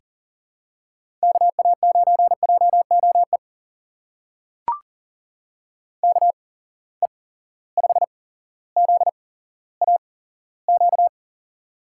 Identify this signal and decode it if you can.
Morse code, heard as a keyed tone, about 40 wpm, spaced as for 17 wpm, 700 Hz. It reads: KA9JOE KE5ZAQ
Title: USA Call Signs - 6 Characters - 8x Character Spacing - 40wpm